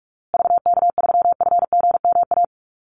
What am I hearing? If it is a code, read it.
Morse code, heard as a keyed tone, about 44 wpm, 700 Hz. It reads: VK3FGMA
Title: USA Call Signs Repeated 2x 45wpm